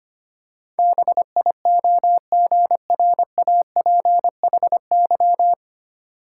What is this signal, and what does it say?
Morse code, heard as a keyed tone, about 25 wpm, 700 Hz. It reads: BIOGRAPHY
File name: practice / 25wpm / Difficult Words with Hints - 25wpm